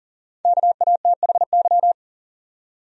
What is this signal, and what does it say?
Morse code, heard as a keyed tone, about 40 wpm, 700 Hz. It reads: KATHY
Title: USA Common Names - Max 5 Letters - 40wpm